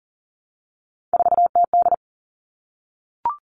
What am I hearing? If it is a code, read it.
Morse code, heard as a keyed tone, about 40 wpm, 700 Hz. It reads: 4TD
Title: All Cut Numbers - 3-Digit 40wpm